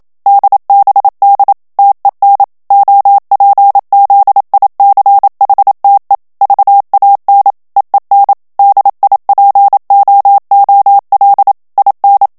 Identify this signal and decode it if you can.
Morse code, heard as a keyed tone, about 27 wpm, 800 Hz. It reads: DBD TEN OPZICHTE VAN EEN DIPOOL IN
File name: HOLLANDAIS texte clair 1 vitesse 26 mots minute